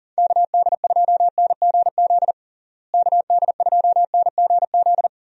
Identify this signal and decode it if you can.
Morse code, heard as a keyed tone, about 40 wpm, 700 Hz. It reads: KD2NGZ KD2NGZ
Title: USA Call Signs - 6 Characters - Repeated 2x - 40wpm